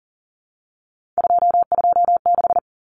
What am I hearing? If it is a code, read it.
Morse code, heard as a keyed tone, about 40 wpm, 700 Hz. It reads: 226
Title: Numbers - 3-Digit 40wpm